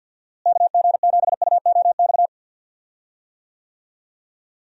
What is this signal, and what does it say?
Morse code, heard as a keyed tone, about 50 wpm, 700 Hz. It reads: KG7UOX
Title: USA Call Signs - 6 Characters - Repeated 2x - 50wpm